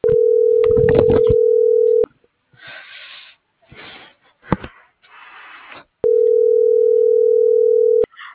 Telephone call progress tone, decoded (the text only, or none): ringback tone